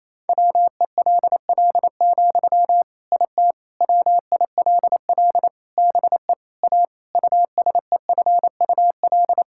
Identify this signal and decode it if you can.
Morse code, heard as a keyed tone, about 28 wpm, 700 Hz. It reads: WELL, IT WILL BE A USEFUL
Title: Sentences from Top 1000 Words - Review Part 6 - 28wpm